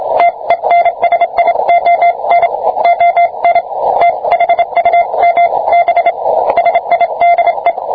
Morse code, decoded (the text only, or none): TENSION ON THUMB SIDE